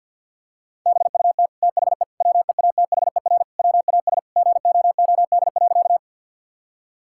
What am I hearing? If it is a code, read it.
Morse code, heard as a keyed tone, about 50 wpm, 700 Hz. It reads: BUT THE WEATHER WAS GOOD.
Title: Sentences from Top 1000 Words - Review Part 5 - 50wpm